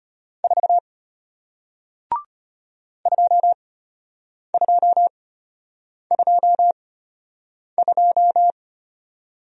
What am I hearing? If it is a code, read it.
Morse code, heard as a keyed tone, about 34 wpm, spaced as for 8 wpm, 700 Hz. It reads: 4 2222